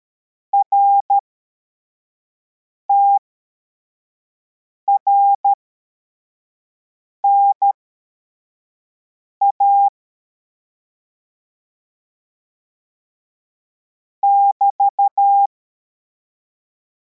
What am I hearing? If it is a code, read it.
Morse code, heard as a keyed tone, about 13 wpm, spaced as for 4 wpm, 800 Hz. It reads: RTRNA <BT>